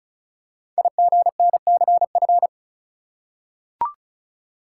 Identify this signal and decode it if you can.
Morse code, heard as a keyed tone, about 36 wpm, 690 Hz. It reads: IGNCF
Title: Code Groups - 5 Letters - 35wpm